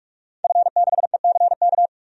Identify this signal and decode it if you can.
Morse code, heard as a keyed tone, about 43 wpm, 700 Hz. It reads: F6ECK
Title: DXCC-100 Call Signs - 5 Characters - 45wpm